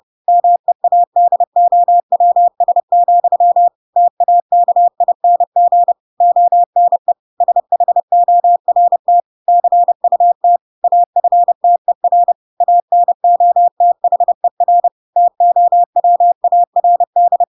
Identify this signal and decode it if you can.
Morse code, heard as a keyed tone, about 30 wpm, 700 Hz. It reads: MEADOWS, TAKING ONE SHORT CUT AFTER ANOTHER TOWARD